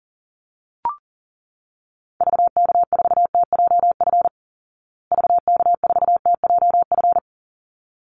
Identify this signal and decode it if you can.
Morse code, heard as a keyed tone, about 40 wpm, 700 Hz. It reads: VK4TJF VK4TJF
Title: Call Signs Repeated 3x - 40wpm